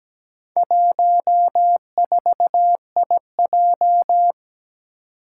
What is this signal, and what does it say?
Morse code, heard as a keyed tone, about 17 wpm, 700 Hz. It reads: 14IJ
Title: Four Challenging Letter-Number Combinations 17wpm